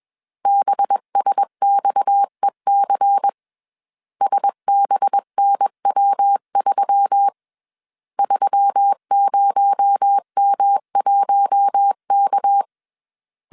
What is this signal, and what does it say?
Morse code, heard as a keyed tone, about 21 wpm, 800 Hz. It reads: BSXEC SBNW3 30M1K
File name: CW / DET / 1 6 1500C S A 20wpm 2048